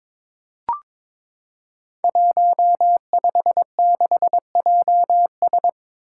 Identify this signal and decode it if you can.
Morse code, heard as a keyed tone, about 22 wpm, 700 Hz. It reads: 156JS
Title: Five Challenging Letter-Number Combinations 22wpm